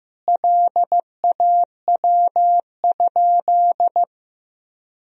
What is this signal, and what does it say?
Morse code, heard as a keyed tone, about 15 wpm, 700 Hz. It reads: LAW?
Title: Sentences from Top 1000 Words - Review Part 4 - 15wpm